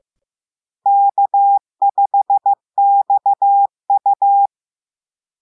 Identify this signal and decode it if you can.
Morse code, heard as a keyed tone, about 15 wpm, 800 Hz. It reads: K5XU